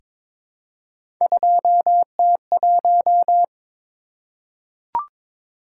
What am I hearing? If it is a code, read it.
Morse code, heard as a keyed tone, about 22 wpm, 700 Hz. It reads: 2T1
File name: practice / 22wpm / Three Letter-Number Combinations - 22wpm